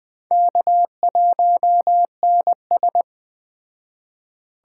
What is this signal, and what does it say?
Morse code, heard as a keyed tone, about 20 wpm, 700 Hz. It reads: K1NS
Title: Speed Racing - USA 1x2 Call Signs 17wpm